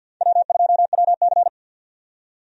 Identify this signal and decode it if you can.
Morse code, heard as a keyed tone, about 51 wpm, 700 Hz. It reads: W2WC
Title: USA Call Signs - 4 Characters - 50wpm